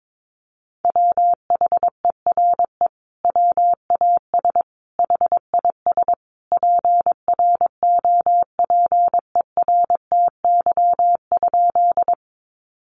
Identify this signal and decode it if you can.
Morse code, heard as a keyed tone, about 22 wpm, 700 Hz. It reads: WHERE WAS HIS PROPERTY?
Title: Sentences from Top 1000 Words 22wpm